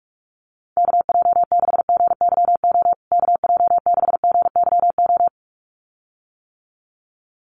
Morse code, heard as a keyed tone, about 44 wpm, 690 Hz. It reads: KJ6GYO KJ6GYO
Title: USA Call Signs Repeated 2x 45wpm